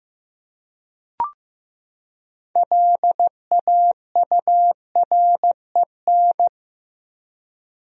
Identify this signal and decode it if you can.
Morse code, heard as a keyed tone, about 15 wpm, 700 Hz. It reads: LAUREN
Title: USA Common Names - 15wpm